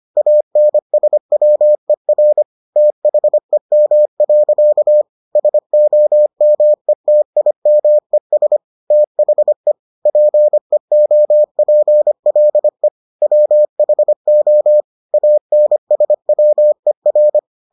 Morse code, heard as a keyed tone, about 25 wpm, 600 Hz. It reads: ANSWER THEM. SOMETIMES THE PEOPLE WHO ANSWER